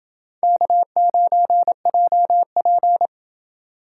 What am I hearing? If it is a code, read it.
Morse code, heard as a keyed tone, about 28 wpm, 700 Hz. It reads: K9JP